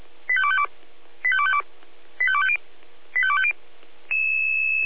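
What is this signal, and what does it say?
Five-tone selective call (ZVEI), heard as quick runs of five tones, tones of about 70 ms. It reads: 85252 85252 85250 85250
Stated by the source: ZVEI five-tone selective call 85259 received by SDR.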